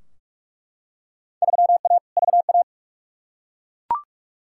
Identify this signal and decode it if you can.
Morse code, heard as a keyed tone, about 42 wpm, 700 Hz. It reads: 3A VA
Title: Field Day Exchanges - 45wpm